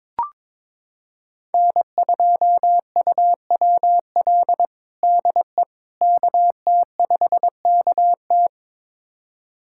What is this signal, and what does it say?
Morse code, heard as a keyed tone, about 22 wpm, 700 Hz. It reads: N2UWL DE KT5KT
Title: USA Callsign DE Callsign - 5 Characters - 22wpm